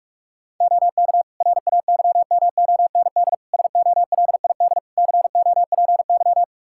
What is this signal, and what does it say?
Morse code, heard as a keyed tone, about 42 wpm, 700 Hz. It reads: OK RAYMOND SOLID COPY
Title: Practice QSO - Exchange 3 - More Info - 45wpm